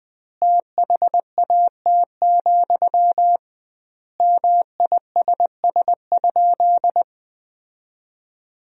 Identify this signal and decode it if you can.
Morse code, heard as a keyed tone, about 21 wpm, 700 Hz. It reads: THAT, MISS?
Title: 2x Word-Spacing - Sentences from Top 500 Words - 20wpm